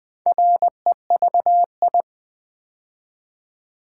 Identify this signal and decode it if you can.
Morse code, heard as a keyed tone, about 20 wpm, 700 Hz. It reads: REVI
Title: Build A Word - 2 Letters At A Time - 8-Letter Words - 20wpm